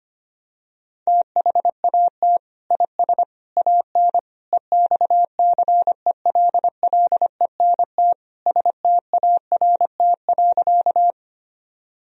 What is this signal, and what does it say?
Morse code, heard as a keyed tone, about 25 wpm, 700 Hz. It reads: THAT IS AN EXCELLENT START.
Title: Sentences from Top 1000 Words - Review Part 1 - 25wpm